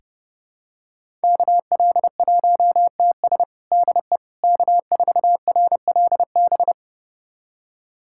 Morse code, heard as a keyed tone, about 30 wpm, 700 Hz. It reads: KL1TS DE K4RLB
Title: USA Callsign DE Callsign 30wpm